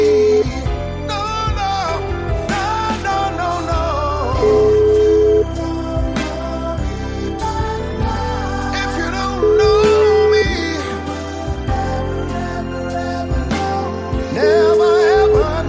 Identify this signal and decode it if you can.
Ringback tone (Europe, 425 Hz), with noise.